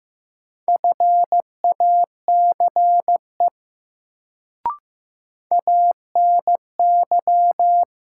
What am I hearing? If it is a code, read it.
Morse code, heard as a keyed tone, about 15 wpm, 700 Hz. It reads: FACE ANY
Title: Sets of 3 Words - New Word Drill - 15wpm